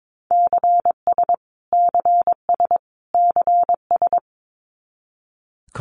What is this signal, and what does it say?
Morse code, heard as a keyed tone, about 22 wpm, 700 Hz. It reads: CS CS CS
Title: Abbreviations Repeated 3x - 22wpm